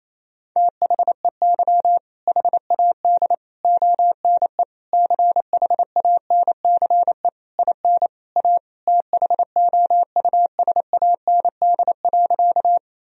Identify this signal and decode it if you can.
Morse code, heard as a keyed tone, about 28 wpm, 700 Hz. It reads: THEY HAD ONE CHANCE IN A THOUSAND.
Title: Sentences from Top 1000 Words - Review Part 3 - 28wpm